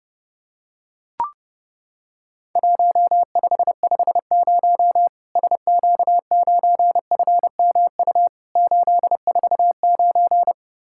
Morse code, heard as a keyed tone, about 30 wpm, 700 Hz. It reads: 1550 SQ9FMU 849
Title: Worked All Europe DX Contest - QTC Exchange - 30wpm